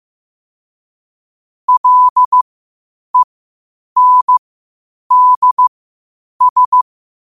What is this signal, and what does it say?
Morse code, heard as a keyed tone, about 15 wpm, spaced as for 8 wpm, 1000 Hz. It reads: LENDS
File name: qotdfiles / 8wpm